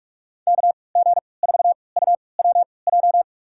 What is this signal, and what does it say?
Morse code, heard as a keyed tone, about 46 wpm, spaced as for 26 wpm, 700 Hz. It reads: KG4UWJ
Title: USA Call Signs - 6 Characters - 3x Character Spacing - 45wpm